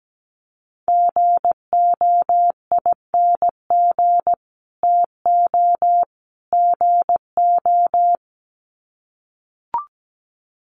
Morse code, heard as a keyed tone, about 17 wpm, 700 Hz. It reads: GOING TO GO
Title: Sets of 3 Words 17wpm